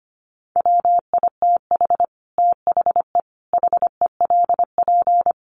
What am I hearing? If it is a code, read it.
Morse code, heard as a keyed tone, about 25 wpm, 700 Hz. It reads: WITH THE HELP
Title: Sets of 4 Words 25wpm